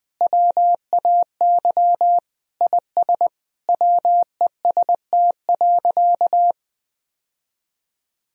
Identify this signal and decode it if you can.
Morse code, heard as a keyed tone, about 20 wpm, 700 Hz. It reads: WAY IS WEST.
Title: Sentences from Top 1000 Words - Review Part 3 - 20wpm